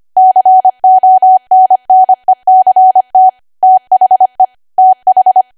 Morse code, heard as a keyed tone, about 25 wpm, 750 Hz. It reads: CONNECT THE TH
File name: Archive / 25 WPM / 25WPM